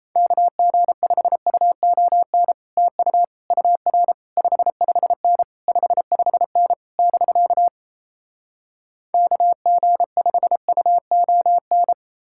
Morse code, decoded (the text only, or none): KG5UON TU UR 55N 55N <BK> KG5UON